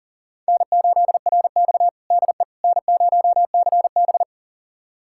Morse code, heard as a keyed tone, about 40 wpm, 700 Hz. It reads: N8RX DE N0CB
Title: USA Callsign DE Callsign - 4 Characters - 40wpm